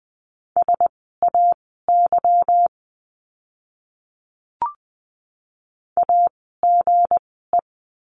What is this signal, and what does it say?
Morse code, heard as a keyed tone, about 20 wpm, spaced as for 14 wpm, 700 Hz. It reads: SAY AGE
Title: 160 Lesson 26f - Letter g - Words 20wpm